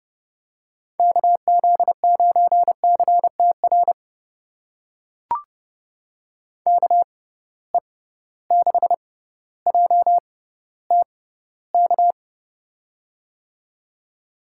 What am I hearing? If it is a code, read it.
Morse code, heard as a keyed tone, about 30 wpm, spaced as for 15 wpm, 700 Hz. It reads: KZ9CTR KE6JTK
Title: USA Call Signs - 6x Character Spacing - 30wpm